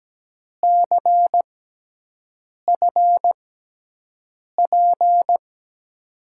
Morse code, heard as a keyed tone, about 17 wpm, spaced as for 6 wpm, 700 Hz. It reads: CFP
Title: USA Call Signs - 6x Character Spacing - 17wpm